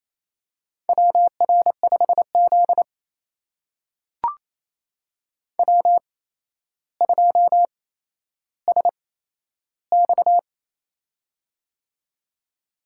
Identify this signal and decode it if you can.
Morse code, heard as a keyed tone, about 28 wpm, spaced as for 12 wpm, 700 Hz. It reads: WR5Z W2SX
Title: USA Call Signs - 4 Characters - 8x Character Spacing - 28wpm